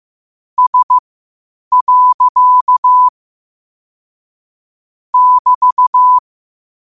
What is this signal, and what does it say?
Morse code, heard as a keyed tone, about 15 wpm, spaced as for 8 wpm, 1000 Hz. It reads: S. <BT>
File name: qotdfiles / 8wpm